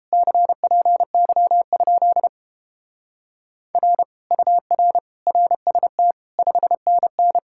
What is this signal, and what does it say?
Morse code, heard as a keyed tone, about 29 wpm, 700 Hz. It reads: CPY? R UR RST 5NN